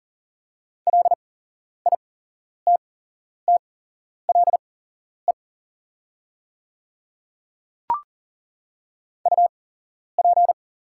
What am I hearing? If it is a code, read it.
Morse code, heard as a keyed tone, about 40 wpm, spaced as for 11 wpm, 700 Hz. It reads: LITTLE UP
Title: 8x Character Spacing - Top 200 Words - 40wpm